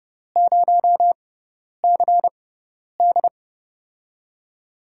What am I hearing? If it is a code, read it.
Morse code, heard as a keyed tone, about 30 wpm, spaced as for 10 wpm, 700 Hz. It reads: 0CD
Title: USA Call Signs - 6x Character Spacing - 30wpm